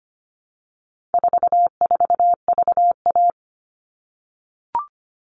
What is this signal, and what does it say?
Morse code, heard as a keyed tone, about 25 wpm, 700 Hz. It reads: 44VA
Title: All Cut Numbers - 4-Digit 25wpm